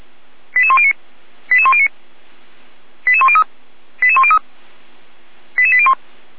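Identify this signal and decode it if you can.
Five-tone selective call (ZVEI), heard as quick runs of five tones, tones of about 70 ms. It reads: 88109 88109 88103 88103 88801
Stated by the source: ZVEI five-tone call received by SDR, recorded by Feuerwehr Orlamünde.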